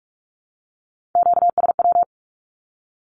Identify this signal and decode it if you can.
Morse code, heard as a keyed tone, about 44 wpm, 700 Hz. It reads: QSW
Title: Q-Codes 45wpm